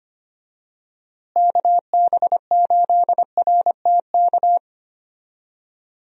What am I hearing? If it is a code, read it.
Morse code, heard as a keyed tone, about 25 wpm, 700 Hz. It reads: KB8RTK
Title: USA Call Signs - 6 Characters - Repeated 2x - 25wpm